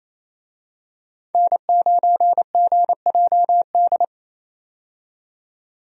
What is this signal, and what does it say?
Morse code, heard as a keyed tone, about 28 wpm, 700 Hz. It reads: N9GJD
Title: USA Call Signs - 5 Characters - 28wpm